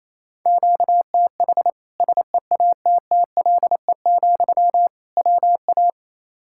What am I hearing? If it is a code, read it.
Morse code, heard as a keyed tone, about 28 wpm, 700 Hz. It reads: QTH SEATTLE, WA
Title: Practice QSO - Exchange 1 - Beginning - 28wpm